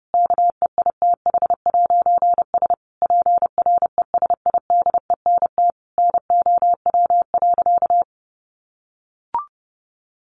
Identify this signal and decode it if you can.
Morse code, heard as a keyed tone, about 30 wpm, 700 Hz. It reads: KEITH'S PRESIDENT NOW.
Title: A Sound Of Thunder 30wpm